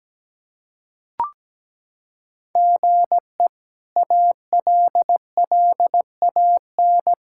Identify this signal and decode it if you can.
Morse code, heard as a keyed tone, about 17 wpm, 700 Hz. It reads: GE ALLAN
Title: K1USN SST QSO Practice - 17wpm